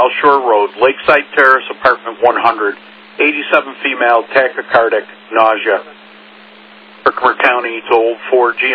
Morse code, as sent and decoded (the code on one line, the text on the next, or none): none